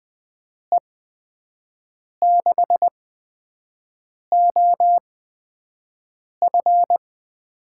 Morse code, heard as a keyed tone, about 20 wpm, spaced as for 5 wpm, 700 Hz. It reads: E6OF
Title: USA Call Signs - 6 Characters - 8x Character Spacing - 20wpm